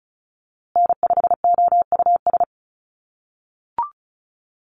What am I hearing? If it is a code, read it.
Morse code, heard as a keyed tone, about 34 wpm, 700 Hz. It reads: N5OUS
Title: USA Call Signs Repeated 2x 35wpm